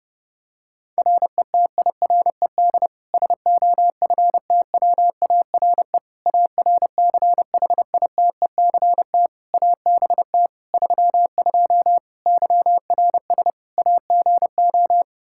Morse code, heard as a keyed tone, about 30 wpm, 700 Hz. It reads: RETIRED SOFTWARE ARCHITECT ABT 32 YRS AGO
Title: Practice QSO - Exchange 3 - More Info - 30wpm